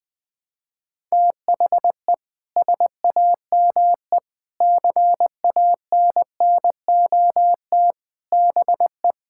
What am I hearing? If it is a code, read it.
Morse code, heard as a keyed tone, about 20 wpm, 700 Hz. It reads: THE SAME CANNOT BE
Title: Sentences with 7-Word Phrases - Spoken Beginning - 20wpm